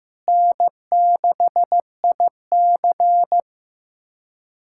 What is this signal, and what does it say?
Morse code, heard as a keyed tone, about 15 wpm, 700 Hz. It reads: N6IC
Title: USA Callsign DE Callsign - 4 Characters - 15wpm